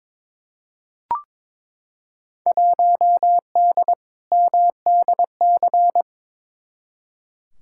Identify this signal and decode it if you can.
Morse code, heard as a keyed tone, about 22 wpm, 700 Hz. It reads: 1D MDC
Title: Field Day Exchanges - 22wpm